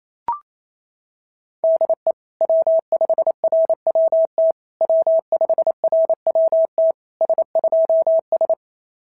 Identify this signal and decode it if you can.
Morse code, heard as a keyed tone, about 25 wpm, 650 Hz. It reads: DE W5RWT W5RWT S2S
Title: SOTA QSO Practice 28wpm